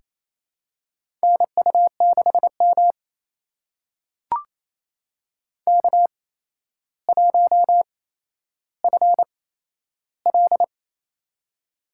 Morse code, heard as a keyed tone, about 28 wpm, spaced as for 12 wpm, 700 Hz. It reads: NU6M K1FL
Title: USA Call Signs - 4 Characters - 8x Character Spacing - 28wpm